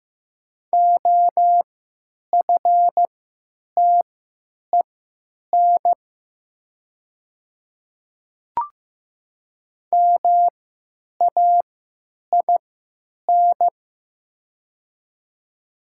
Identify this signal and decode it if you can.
Morse code, heard as a keyed tone, about 15 wpm, spaced as for 9 wpm, 700 Hz. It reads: OFTEN MAIN
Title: 3x Character Spacing - Top 500 Words - 15wpm